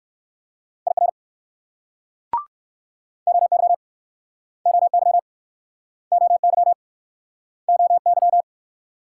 Morse code, heard as a keyed tone, about 59 wpm, 690 Hz. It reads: IR OY OY OY OY